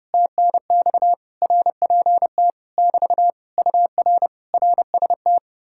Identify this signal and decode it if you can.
Morse code, heard as a keyed tone, about 33 wpm, 700 Hz. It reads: TNX RPT <BT> UR RST